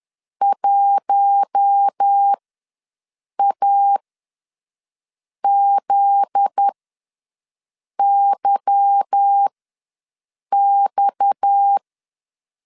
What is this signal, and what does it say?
Morse code, heard as a keyed tone, about 10 wpm, spaced as for 5 wpm, 800 Hz. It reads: 1AZYX
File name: LEZIONI / 9 28 ws 10wpm 4124